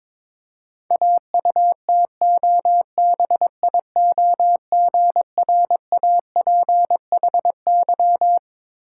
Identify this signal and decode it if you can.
Morse code, heard as a keyed tone, about 21 wpm, 690 Hz. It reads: AUTOBIOGRAPHY
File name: Difficult Words with Hints - 22wpm